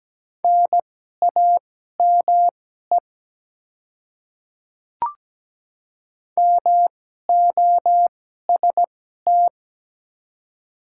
Morse code, heard as a keyed tone, about 17 wpm, spaced as for 12 wpm, 700 Hz. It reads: NAME MOST
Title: 2x Character Spacing - Top 200 Words - 17wpm